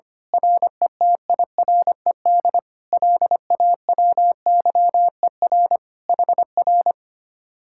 Morse code, heard as a keyed tone, about 24 wpm, 700 Hz. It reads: RETIRED LAWYER HR
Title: Speed-Racing Common QSO Phrases - 25wpm